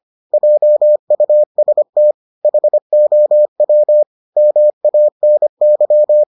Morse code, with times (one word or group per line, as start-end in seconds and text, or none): 0.3-2.1 JUST
2.4-4.0 HOW
4.4-6.2 MANY